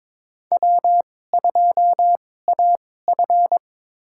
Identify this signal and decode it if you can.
Morse code, heard as a keyed tone, about 22 wpm, spaced as for 16 wpm, 700 Hz. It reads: W2AF
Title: USA Call Signs - 4 Characters - 2x Character Spacing - 22wpm